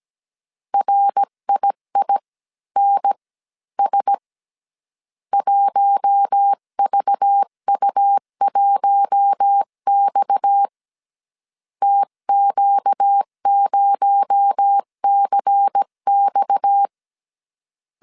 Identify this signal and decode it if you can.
Morse code, heard as a keyed tone, about 17 wpm, spaced as for 14 wpm, 800 Hz. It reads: RIINS 1VU1X TQ0CX
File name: CW / DET / 1 6 1500C S C 16wpm 2733